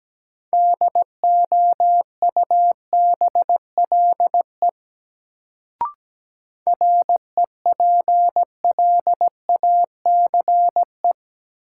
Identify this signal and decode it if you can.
Morse code, heard as a keyed tone, about 17 wpm, 700 Hz. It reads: DOUBLE REPLACE
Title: Sentences from Top 1400 Words - New Word Drill Alt - 17wpm